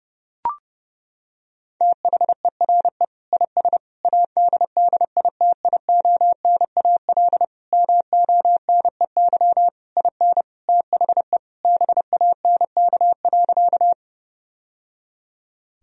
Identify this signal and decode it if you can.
Morse code, heard as a keyed tone, about 32 wpm, 700 Hz. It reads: THERE IS ADDITIONAL MONEY IN THE BANK.
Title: Sentences from Top 1000 Words - Review Part 4 - 30wpm